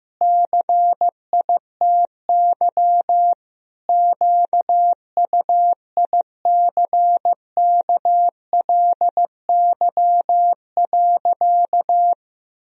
Morse code, heard as a keyed tone, about 15 wpm, 700 Hz. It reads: CITY QUICKLY.
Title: Sentences from Top 1000 Words - Review Part 1 - 15wpm